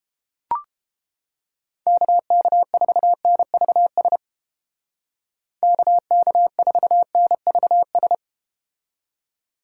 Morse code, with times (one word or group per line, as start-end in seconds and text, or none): 1.9-4.2 KK4NVS
5.6-8.2 KK4NVS